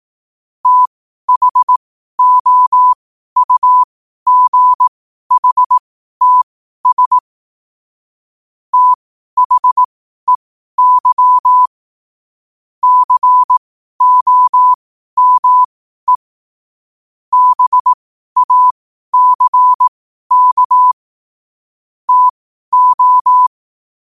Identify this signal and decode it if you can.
Morse code, heard as a keyed tone, about 18 wpm, spaced as for 12 wpm, 1000 Hz. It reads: THOUGHTS THEY COME BACK TO
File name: qotdfiles / 12wpm